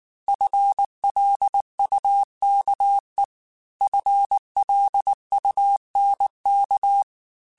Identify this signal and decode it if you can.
Morse code, heard as a keyed tone, about 19 wpm, 800 Hz. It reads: FLUKE FLUNK
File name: FL Runs / FL Runs mp3s / FL Runs @18